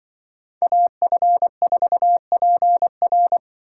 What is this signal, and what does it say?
Morse code, heard as a keyed tone, about 23 wpm, 700 Hz. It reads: AF4PR
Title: Speed Racing - USA Call Signs 20wpm